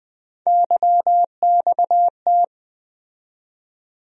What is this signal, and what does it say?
Morse code, heard as a keyed tone, about 20 wpm, 700 Hz. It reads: YXT
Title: USA Call Sign Postfixes - 3 Letters 20wpm